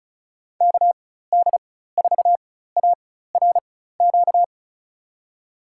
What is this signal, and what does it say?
Morse code, heard as a keyed tone, about 36 wpm, spaced as for 16 wpm, 700 Hz. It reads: KD4ARQ
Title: USA Call Signs - 6 Characters - 4x Character Spacing - 35wpm